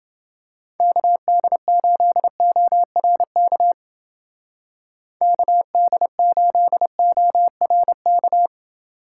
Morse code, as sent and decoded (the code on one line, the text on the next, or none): -.- -.. ---.. --- .-. -.- / -.- -.. ---.. --- .-. -.-
KD8ORK KD8ORK